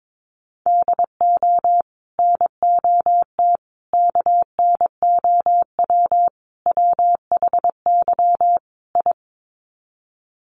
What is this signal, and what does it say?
Morse code, heard as a keyed tone, about 22 wpm, 700 Hz. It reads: DO NOT KNOW WHY I
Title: Sets of 5 Words Encore 22wpm